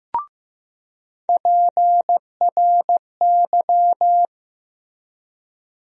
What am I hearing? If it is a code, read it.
Morse code, heard as a keyed tone, about 15 wpm, 700 Hz. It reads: PRY
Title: USA Call Sign Postfixes - 3 Letters 15wpm